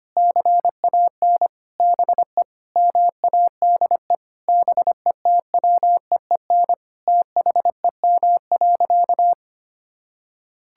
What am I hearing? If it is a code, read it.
Morse code, heard as a keyed tone, about 25 wpm, 700 Hz. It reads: CAN BE MADE BETWEEN THEM.
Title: Sentences from Top 1000 Words - Review Part 6 - 25wpm